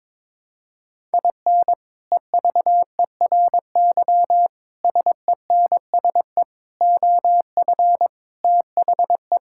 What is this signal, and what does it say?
Morse code, heard as a keyed tone, about 22 wpm, 700 Hz. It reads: IN EVERY SENSE OF THE
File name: practice / 22wpm / Sets of 6 Words - 22wpm